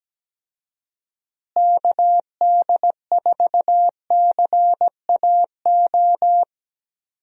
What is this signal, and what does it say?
Morse code, heard as a keyed tone, about 17 wpm, 700 Hz. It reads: KD4CAO
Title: USA Call Signs - 6 Characters - Repeated 2x - 17wpm